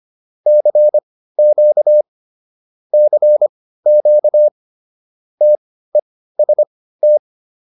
Morse code, heard as a keyed tone, about 25 wpm, spaced as for 15 wpm, 600 Hz. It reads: CQ CQ TEST